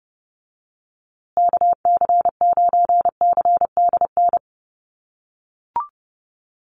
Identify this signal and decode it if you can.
Morse code, heard as a keyed tone, about 30 wpm, 700 Hz. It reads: KC9CDN
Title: USA Call Signs Repeated 2x 30wpm